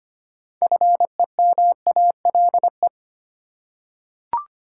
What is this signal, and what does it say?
Morse code, heard as a keyed tone, about 25 wpm, 700 Hz. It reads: FEMALE
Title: Six Letter Words - 25wpm